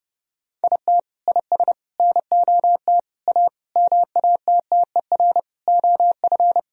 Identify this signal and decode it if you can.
Morse code, heard as a keyed tone, about 30 wpm, 700 Hz. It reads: IT IS NOT A MATTER OF
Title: Sentences from Top 1000 Words - Review Part 4 - 30wpm